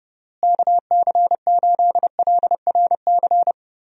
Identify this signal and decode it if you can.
Morse code, heard as a keyed tone, about 30 wpm, 700 Hz. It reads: KC8LRC
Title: USA Call Signs - 6 Characters - Repeated 3x - 30wpm